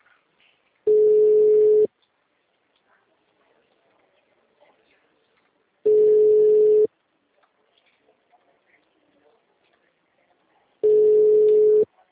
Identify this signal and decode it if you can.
Ringback tone (Europe, 425 Hz).
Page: Limeira 4 - SP